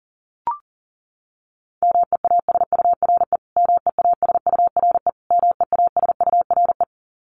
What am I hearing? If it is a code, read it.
Morse code, heard as a keyed tone, about 37 wpm, 700 Hz. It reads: MEASURE MEASURE MEASURE
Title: Top 500 Words Repeated 40wpm